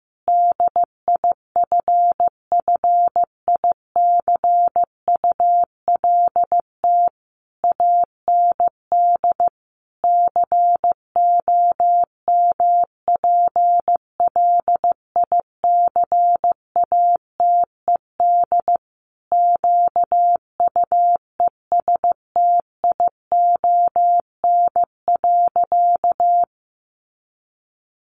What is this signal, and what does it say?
Morse code, heard as a keyed tone, about 15 wpm, 700 Hz. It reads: DIFFICULT AND COMPLICATED QUESTION.
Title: Sentences from Top 1200 Words 15wpm